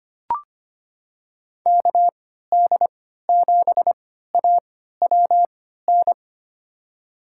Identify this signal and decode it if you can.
Morse code, heard as a keyed tone, about 25 wpm, spaced as for 14 wpm, 690 Hz. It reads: KD7AWN
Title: USA Call Signs - 6 Characters - 3x Character Spacing - 25wpm